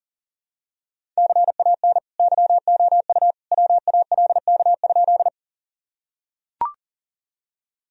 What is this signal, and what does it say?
Morse code, heard as a keyed tone, about 40 wpm, 700 Hz. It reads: CAN YOU WALK?
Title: Sentences from Top 1000 Words - Review Part 1 - 40wpm